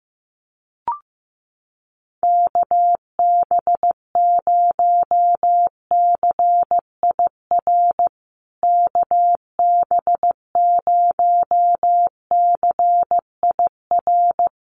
Morse code, heard as a keyed tone, about 15 wpm, 700 Hz. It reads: KB0CIR KB0CIR
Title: USA Call Signs Repeated 3x 15wpm